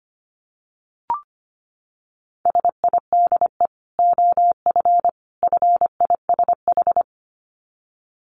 Morse code, heard as a keyed tone, about 25 wpm, 700 Hz. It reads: SIDE OF FISH
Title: Dit Challenge 25wpm